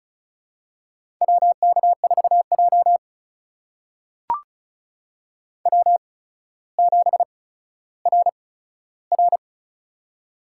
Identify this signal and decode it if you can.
Morse code, heard as a keyed tone, about 36 wpm, spaced as for 15 wpm, 700 Hz. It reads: WK4J W7RR
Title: USA Call Signs - 4 Characters - 8x Character Spacing - 35wpm